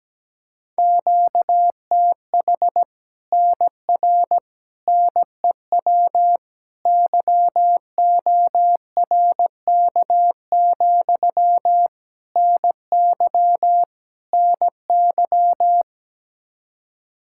Morse code, heard as a keyed tone, about 17 wpm, 700 Hz. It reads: QTH NR NEW YORK, NY NY
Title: Practice QSO - Exchange 1 - Beginning - 17wpm